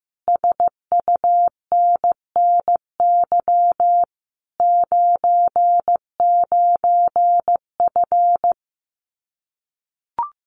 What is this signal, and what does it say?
Morse code, heard as a keyed tone, about 15 wpm, 700 Hz. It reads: SUNNY 99F
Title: Common QSO Phrases Encore 15wpm